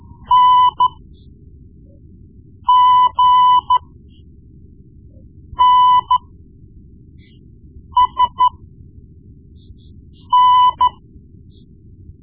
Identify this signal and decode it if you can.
Morse code, heard as a keyed tone, about 13 wpm, spaced as for 4 wpm, 1000 Hz. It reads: NGNSN
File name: Morse Lessons / Morse Lessons / Tape-1B